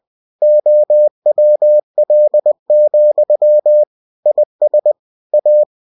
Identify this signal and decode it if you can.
Morse code, heard as a keyed tone, about 20 wpm, 600 Hz. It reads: OWL, IS A
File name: cw-books / winnie-the-pooh / winnie-20wpm-page-0082